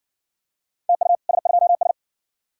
Morse code, heard as a keyed tone, about 60 wpm, 700 Hz. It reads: TU S2S